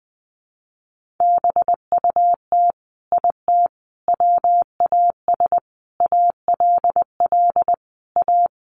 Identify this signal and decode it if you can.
Morse code, heard as a keyed tone, about 20 wpm, 700 Hz. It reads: BUT IT WAS ALL A
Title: Sentences from Top 1500 Words 20wpm